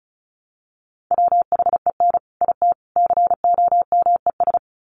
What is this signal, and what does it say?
Morse code, heard as a keyed tone, about 34 wpm, 700 Hz. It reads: WHEN IT COMES
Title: Sets of 3 Words 35wpm